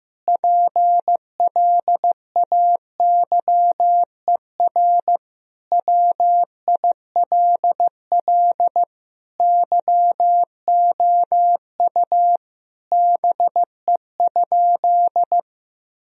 Morse code, heard as a keyed tone, about 15 wpm, 700 Hz. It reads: PLAYER WILL YOU BE?
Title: Sentences from Top 1000 Words - Review Part 4 - 15wpm